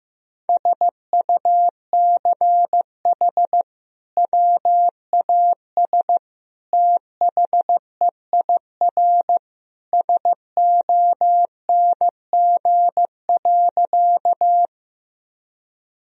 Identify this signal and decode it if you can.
Morse code, heard as a keyed tone, about 15 wpm, 700 Hz. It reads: SUCH WAS THEIR SONG.
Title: Sentences from Top 1000 Words - Review Part 2 - 15wpm